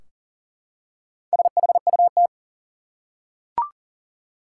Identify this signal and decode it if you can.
Morse code, heard as a keyed tone, about 40 wpm, 700 Hz. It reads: SHUT
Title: Four Letter Words - 40wpm